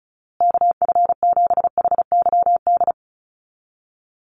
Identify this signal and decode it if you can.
Morse code, heard as a keyed tone, about 34 wpm, 700 Hz. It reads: KF7HYD
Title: USA Call Signs Repeated 2x 35wpm